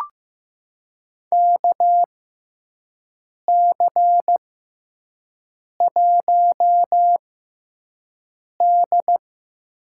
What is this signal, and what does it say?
Morse code, heard as a keyed tone, about 15 wpm, spaced as for 5 wpm, 700 Hz. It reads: KC1D
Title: USA Call Signs - 6x Character Spacing - 15wpm